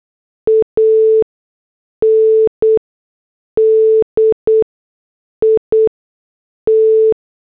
Morse code, heard as a keyed tone, about 8 wpm, spaced as for 6 wpm, 440 Hz. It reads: AND IT